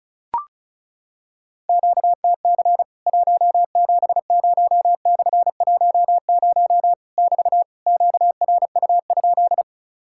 Morse code, heard as a keyed tone, about 36 wpm, 700 Hz. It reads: QTC 170/10 <BT> QRU?
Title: Worked All Europe DX Contest - QTC Exchange - 35wpm